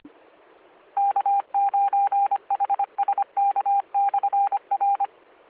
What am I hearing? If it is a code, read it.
Morse code, heard as a keyed tone, about 25 wpm, 800 Hz. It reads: K9HSK/R